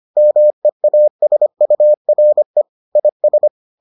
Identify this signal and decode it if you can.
Morse code, heard as a keyed tone, about 25 wpm, 600 Hz. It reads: MEASURE IS